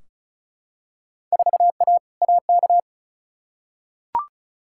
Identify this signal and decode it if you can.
Morse code, heard as a keyed tone, about 36 wpm, 700 Hz. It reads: 4A AK
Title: Field Day Exchanges Encore - 35wpm